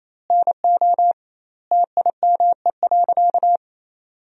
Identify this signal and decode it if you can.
Morse code, heard as a keyed tone, about 28 wpm, 700 Hz. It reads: NO TIME.
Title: 2x Word-Spacing - Sentences from Top 400 Words - 28wpm